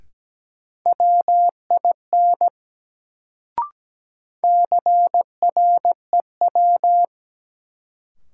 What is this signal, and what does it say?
Morse code, heard as a keyed tone, about 17 wpm, 700 Hz. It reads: WIN CREW
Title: Sentences from Top 1500 Words - New Word Drill Alt - 17wpm